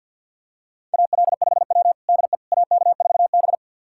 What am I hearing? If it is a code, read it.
Morse code, heard as a keyed tone, about 52 wpm, 690 Hz. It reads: AL5W DE AK4B
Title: USA Callsign DE Callsign - 4 Characters - 50wpm